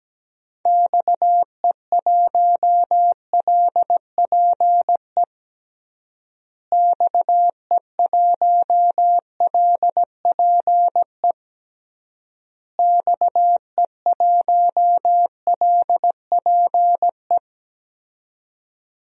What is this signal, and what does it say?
Morse code, heard as a keyed tone, about 17 wpm, 700 Hz. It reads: XE1LPE XE1LPE XE1LPE
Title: DXCC-100 Call Signs - 6 Characters - Repeated 3x - 17wpm